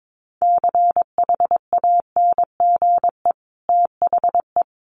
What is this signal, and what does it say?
Morse code, heard as a keyed tone, about 22 wpm, 700 Hz. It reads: CHANGE THE
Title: Sentences from Top 800 Words 22wpm